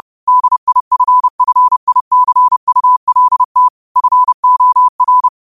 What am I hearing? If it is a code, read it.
Morse code, heard as a keyed tone, about 30 wpm, 1000 Hz. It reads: DIFFICULT FOR
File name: qotdfiles / 30wpm